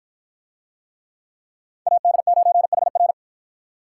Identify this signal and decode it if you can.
Morse code, heard as a keyed tone, about 48 wpm, 700 Hz. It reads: AD9HR